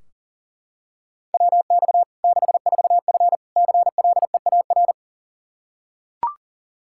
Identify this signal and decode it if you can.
Morse code, heard as a keyed tone, about 37 wpm, 700 Hz. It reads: WX 64F CLEAR
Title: Practice QSO - Exchange 2 - Info - 40wpm